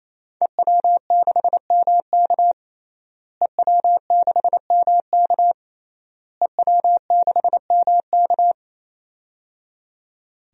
Morse code, heard as a keyed tone, about 28 wpm, 700 Hz. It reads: EW6MK EW6MK EW6MK
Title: DXCC-100 Call Signs - 5 Characters - Repeated 3x - 28wpm